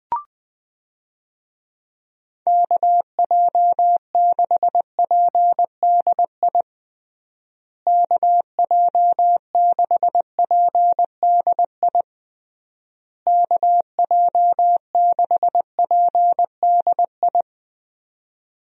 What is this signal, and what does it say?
Morse code, heard as a keyed tone, about 20 wpm, 700 Hz. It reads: KJ6PDI KJ6PDI KJ6PDI
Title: USA Call Signs - 6 Characters - Repeated 3x - 20wpm